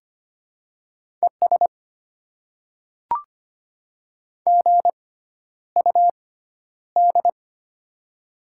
Morse code, heard as a keyed tone, about 25 wpm, spaced as for 11 wpm, 700 Hz. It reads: ES GUD
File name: Top 20 QSO Elements - 6x Character Spacing - 25wpm